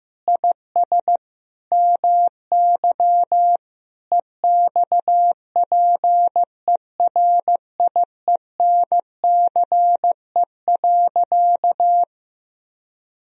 Morse code, heard as a keyed tone, about 15 wpm, 700 Hz. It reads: IS MY EXPERIENCE.
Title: Sentences from Top 1000 Words - Review Part 3 - 15wpm